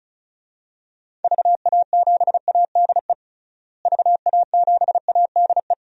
Morse code, heard as a keyed tone, about 36 wpm, 700 Hz. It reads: VA7ADE VA7ADE
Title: DXCC-100 Call Signs - 6 Characters - Repeated 2x - 35wpm